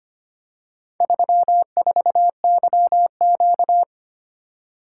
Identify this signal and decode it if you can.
Morse code, heard as a keyed tone, about 25 wpm, 700 Hz. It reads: 34YQ
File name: practice / 25wpm / Four Letter-Number Combinations - 25wpm